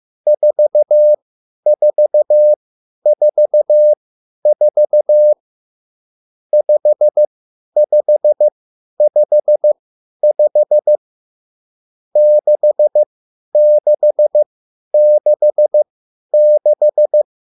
Morse code, heard as a keyed tone, about 15 wpm, spaced as for 10 wpm, 600 Hz. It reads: 4444 5555 6666
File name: QSL / CW lesson